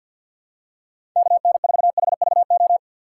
Morse code, heard as a keyed tone, about 48 wpm, 700 Hz. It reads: KN4HVO
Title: USA Call Signs - 6 Characters - 6x Character Spacing - 50wpm